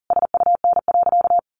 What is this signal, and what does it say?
Morse code, heard as a keyed tone, about 40 wpm, 700 Hz. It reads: SUN.
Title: Sentences from Top 1000 Words 40wpm